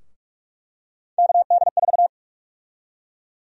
Morse code, heard as a keyed tone, about 44 wpm, 700 Hz. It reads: KD4
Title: USA Call Sign Prefixes - Repeated 2x - 45wpm